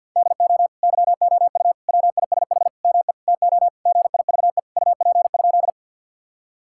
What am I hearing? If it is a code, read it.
Morse code, heard as a keyed tone, about 50 wpm, 700 Hz. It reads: DO YOU WISH ME TO GIVE UP?
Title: Sentences from Top 1600 Words 50wpm